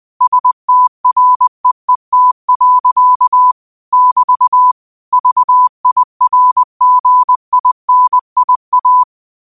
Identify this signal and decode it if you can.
Morse code, heard as a keyed tone, about 20 wpm, 1000 Hz. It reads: STREET. <BT> VIRGINIA
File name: qotdfiles / 20wpm